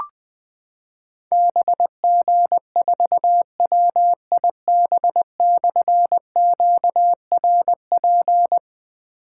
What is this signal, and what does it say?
Morse code, heard as a keyed tone, about 20 wpm, 700 Hz. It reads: BG4WIB/QRP
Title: Difficult Call Signs 20wpm